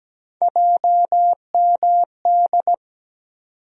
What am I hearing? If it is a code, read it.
Morse code, heard as a keyed tone, about 17 wpm, 700 Hz. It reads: JMD
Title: USA Call Sign Postfixes - 3 Letters 17wpm